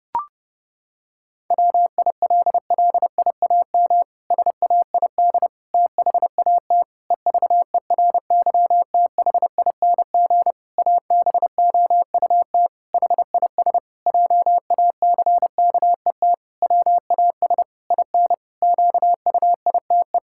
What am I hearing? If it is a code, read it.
Morse code, heard as a keyed tone, about 30 wpm, 700 Hz. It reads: WILLIAM SAID THAT EVERYTHING ABOUT HIS JACKET WAS IN QUITE
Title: Pangram Challenge v2 - 30wpm